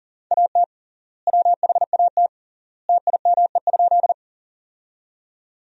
Morse code, heard as a keyed tone, about 37 wpm, 700 Hz. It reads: AT WHAT TIME?
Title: 3x Word-Spacing - Sentences from Top 100 Words - 40wpm